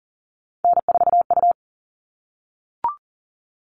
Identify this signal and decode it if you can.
Morse code, heard as a keyed tone, about 40 wpm, 700 Hz. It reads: N4U
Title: All Cut Numbers - 3-Digit 40wpm